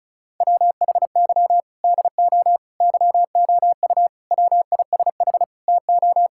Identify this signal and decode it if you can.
Morse code, heard as a keyed tone, about 36 wpm, 700 Hz. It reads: WHY DO YOU WISH TO
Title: Sentences from Top 1600 Words 35wpm